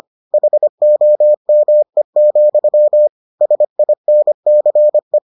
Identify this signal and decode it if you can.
Morse code, heard as a keyed tone, about 25 wpm, 600 Hz. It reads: HOME, SINCE